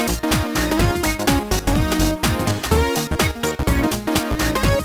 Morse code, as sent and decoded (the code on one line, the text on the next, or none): none